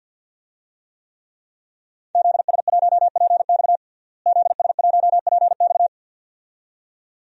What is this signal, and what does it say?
Morse code, heard as a keyed tone, about 49 wpm, 700 Hz. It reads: ZS1PX ZS1PX
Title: DXCC-100 Call Signs - 5 Characters - Repeated 2x - 50wpm